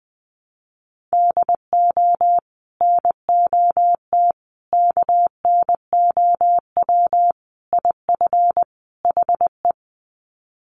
Morse code, heard as a keyed tone, about 20 wpm, 700 Hz. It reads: DO NOT KNOW IF HE
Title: Sets of 5 Words Encore2 20wpm